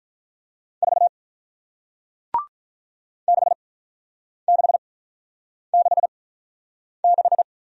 Morse code, heard as a keyed tone, about 46 wpm, spaced as for 9 wpm, 700 Hz. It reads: 4 6666